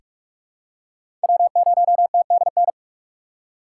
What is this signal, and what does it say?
Morse code, heard as a keyed tone, about 44 wpm, 690 Hz. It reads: W0TDN
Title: USA Call Signs - 5 Characters - 4x Character Spacing - 45wpm